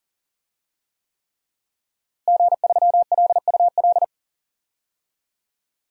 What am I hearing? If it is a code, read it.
Morse code, heard as a keyed tone, about 40 wpm, 700 Hz. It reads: G3LUL
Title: DXCC-100 Call Signs - 5 Characters - Repeated 3x - 40wpm